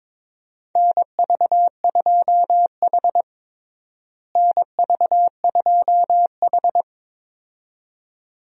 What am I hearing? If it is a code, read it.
Morse code, heard as a keyed tone, about 23 wpm, 700 Hz. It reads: NV2H NV2H
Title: USA Call Signs - 4 Characters - Repeated 2x - 22wpm